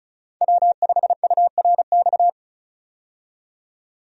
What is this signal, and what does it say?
Morse code, heard as a keyed tone, about 34 wpm, 700 Hz. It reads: W5URX
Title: USA Call Signs - 5 Characters - 35wpm